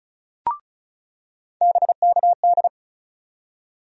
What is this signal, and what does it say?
Morse code, heard as a keyed tone, about 35 wpm, 700 Hz. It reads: BKD
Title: USA Call Sign Postfixes - 3 Letters 35wpm